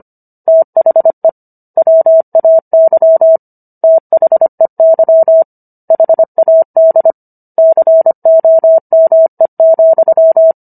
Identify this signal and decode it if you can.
Morse code, heard as a keyed tone, about 25 wpm, 650 Hz. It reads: THE WAY THEY HAD COME,